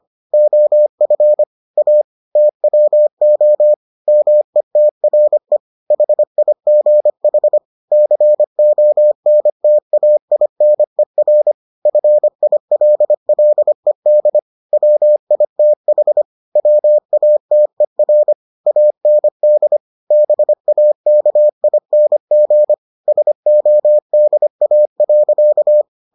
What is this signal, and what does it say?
Morse code, heard as a keyed tone, about 25 wpm, 600 Hz. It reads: OF A TWO METRE HIGH CONTAINER FILLED WITH WATER AND BAKING SODA.